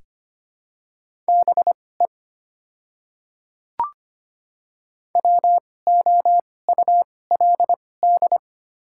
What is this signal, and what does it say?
Morse code, heard as a keyed tone, about 25 wpm, spaced as for 18 wpm, 690 Hz. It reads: BE WOULD
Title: 2x Character Spacing - Top 100 Words - 25wpm